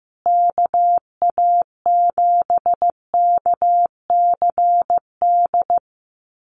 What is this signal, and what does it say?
Morse code, heard as a keyed tone, about 15 wpm, 700 Hz. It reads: KA7KCD
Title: USA Call Signs Repeated 3x 15wpm